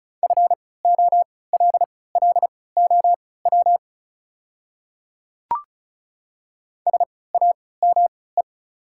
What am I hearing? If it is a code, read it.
Morse code, heard as a keyed tone, about 35 wpm, spaced as for 20 wpm, 690 Hz. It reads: FOLLOW SAME
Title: 3x Character Spacing - Top 200 Words - 35wpm